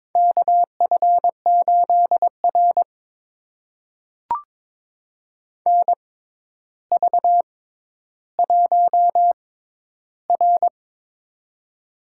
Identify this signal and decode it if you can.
Morse code, heard as a keyed tone, about 23 wpm, spaced as for 11 wpm, 700 Hz. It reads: KF8R NV1R
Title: USA Call Signs - 4 Characters - 6x Character Spacing - 22wpm